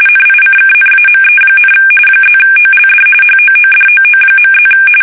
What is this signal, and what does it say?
RTTY, about 45.45 baud. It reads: NICKNAME, +OLD HOODOO.+
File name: fsk850 / battleships